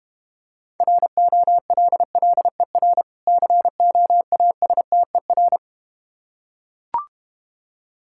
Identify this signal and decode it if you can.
Morse code, heard as a keyed tone, about 32 wpm, 700 Hz. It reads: ROLLER COASTER